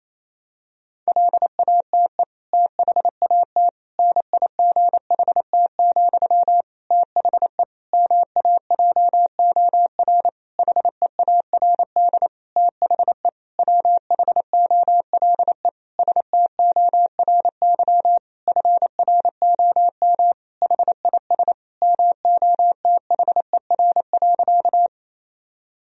Morse code, heard as a keyed tone, about 28 wpm, 700 Hz. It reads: LATE THAT NIGHT, THE MAJOR HEARD THE WHOLE STORY FROM HIS MOTHER.
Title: Sentences from Top 1000 Words - Review Part 4 - 28wpm